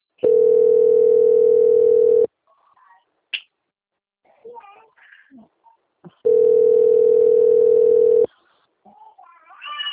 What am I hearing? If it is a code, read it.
Ringback tone (North America), 440 + 480 Hz.